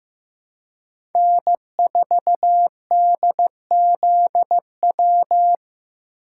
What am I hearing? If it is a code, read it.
Morse code, heard as a keyed tone, about 15 wpm, 700 Hz. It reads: N4DZW
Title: USA Call Signs - 5 Characters - 2x Character Spacing - 15wpm